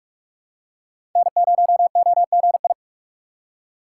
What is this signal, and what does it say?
Morse code, heard as a keyed tone, about 46 wpm, 700 Hz. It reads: N0OGI
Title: USA Call Signs - 5 Characters - 4x Character Spacing - 45wpm